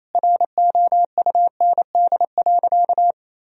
Morse code, heard as a keyed tone, about 28 wpm, 700 Hz. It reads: ROUND.
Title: Sentences from Top 1000 Words - Review Part 1 - 28wpm